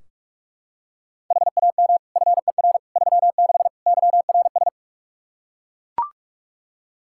Morse code, heard as a keyed tone, about 41 wpm, 700 Hz. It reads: HAM FER 36 YRS
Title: Practice QSO - Exchange 3 - More Info - 45wpm